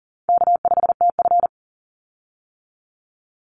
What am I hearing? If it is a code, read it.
Morse code, heard as a keyed tone, about 40 wpm, 700 Hz. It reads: K5TF
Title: Easy Call Signs - 40wpm